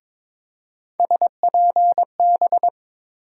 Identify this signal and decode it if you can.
Morse code, heard as a keyed tone, about 22 wpm, 700 Hz. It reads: SPB
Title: USA Call Sign Postfixes - 3 Letters 22wpm